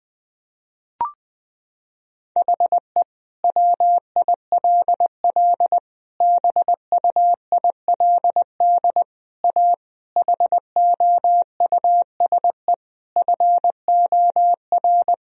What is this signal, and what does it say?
Morse code, heard as a keyed tone, about 20 wpm, 700 Hz. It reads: HE WILL BUILD A HOUSE FOR
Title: Sentences from Top 1000 Words - Review Part 1 - 20wpm